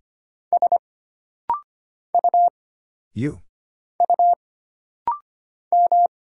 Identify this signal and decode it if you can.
Morse code, heard as a keyed tone, about 25 wpm, 700 Hz. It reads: S U U M
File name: Single Character (Letter:Number:Punctuation:Prosign) - Rapid-Fire - 25wpm